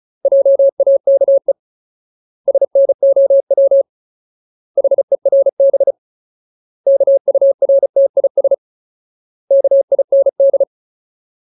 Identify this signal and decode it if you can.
Morse code, heard as a keyed tone, about 34 wpm, 550 Hz. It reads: JAKE SNOW HERB KURTIS KIND
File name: Ultimate Word List / 35WPM Ultimate